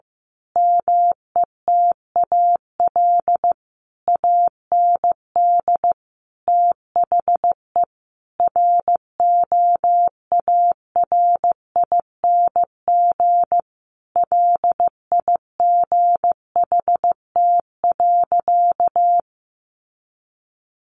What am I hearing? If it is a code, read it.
Morse code, heard as a keyed tone, about 15 wpm, 700 Hz. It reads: METAL AND THE ROARING LIGHT.
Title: A Sound Of Thunder 15wpm